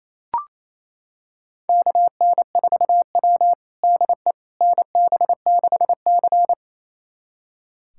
Morse code, heard as a keyed tone, about 27 wpm, 690 Hz. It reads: KN4W DE NB6C
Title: USA Callsign DE Callsign - 4 Characters - 28wpm